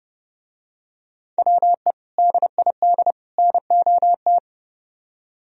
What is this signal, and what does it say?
Morse code, heard as a keyed tone, about 28 wpm, 700 Hz. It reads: WE DID NOT